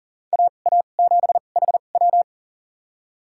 Morse code, heard as a keyed tone, about 40 wpm, spaced as for 29 wpm, 700 Hz. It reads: AA7HW
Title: USA Call Signs - 5 Characters - 2x Character Spacing - 40wpm